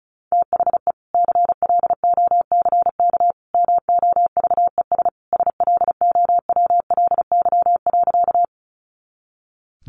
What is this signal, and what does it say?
Morse code, heard as a keyed tone, about 34 wpm, 700 Hz. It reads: THE CLOCK MOVES SLOWLY.
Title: Easy Phrases 35wpm